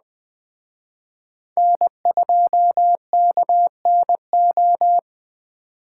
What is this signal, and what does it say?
Morse code, heard as a keyed tone, about 19 wpm, 700 Hz. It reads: N2KNO